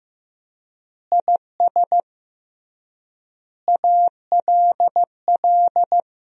Morse code, heard as a keyed tone, about 15 wpm, 700 Hz. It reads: IS ALL
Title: 3x Word-Spacing - Sentences from Top 300 Words - 15wpm